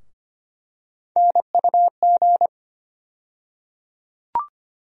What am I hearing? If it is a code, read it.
Morse code, heard as a keyed tone, about 25 wpm, 700 Hz. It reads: NUG
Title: USA Call Sign Postfixes - 3 Letters 25wpm